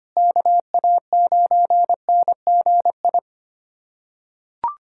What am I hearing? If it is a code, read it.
Morse code, heard as a keyed tone, about 25 wpm, 700 Hz. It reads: KA9NGI
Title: USA Callsign DE Callsign 25wpm